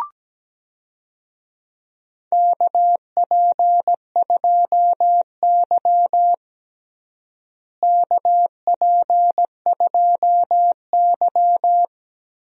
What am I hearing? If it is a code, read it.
Morse code, heard as a keyed tone, about 17 wpm, 700 Hz. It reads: KP2Y KP2Y
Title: DXCC-100 Call Signs - 4 Characters - Repeated 2x - 17wpm